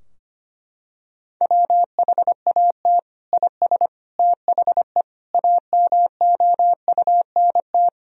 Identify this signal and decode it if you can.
Morse code, heard as a keyed tone, about 25 wpm, 700 Hz. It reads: WHAT IS THE AMOUNT
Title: Sentences from Top 1000 Words - Review Part 4 - 25wpm